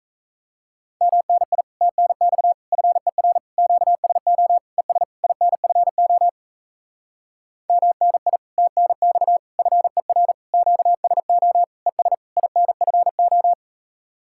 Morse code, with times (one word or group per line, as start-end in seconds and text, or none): 1.0-1.6 MNI
1.8-2.5 TNX
2.7-3.4 FER
3.6-4.6 QSO
4.8-5.0 ES
5.2-6.3 INFO
7.7-8.4 MNI
8.6-9.4 TNX
9.6-10.3 FER
10.5-11.6 QSO
11.9-12.1 ES
12.4-13.5 INFO